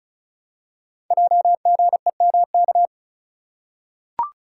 Morse code, heard as a keyed tone, about 35 wpm, 700 Hz. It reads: JGEMK
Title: Code Groups - 5 Letters - 35wpm